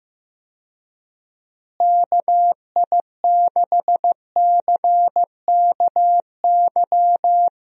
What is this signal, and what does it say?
Morse code, heard as a keyed tone, about 15 wpm, 700 Hz. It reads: KI6CKY
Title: USA Call Signs - 6 Characters - Repeated 3x - 15wpm